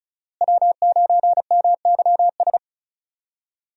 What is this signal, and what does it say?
Morse code, heard as a keyed tone, about 34 wpm, 700 Hz. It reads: W9MYS
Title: USA Call Signs - 5 Characters - Repeated 2x - 35wpm